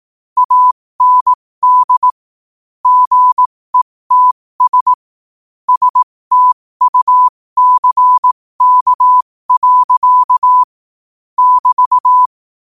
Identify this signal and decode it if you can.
Morse code, heard as a keyed tone, about 18 wpm, spaced as for 15 wpm, 1000 Hz. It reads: AND GETS STUCK. <BT>
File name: qotdfiles / 15wpm